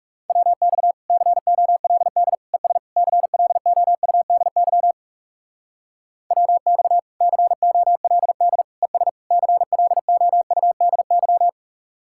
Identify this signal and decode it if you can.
Morse code, heard as a keyed tone, about 38 wpm, 700 Hz. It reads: WX COLD ES CLOUDY WX COLD ES CLOUDY